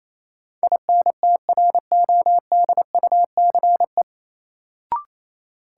Morse code, heard as a keyed tone, about 28 wpm, 700 Hz. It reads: INTRODUCE
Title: Sentences from Top 1600 Words - New Word Drill - 28wpm